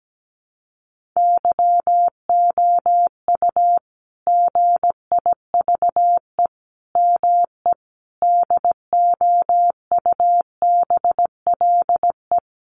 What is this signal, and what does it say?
Morse code, heard as a keyed tone, about 17 wpm, 700 Hz. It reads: YOU GIVE ME DOUBLE
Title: Sentences from Top 1400 Words 17wpm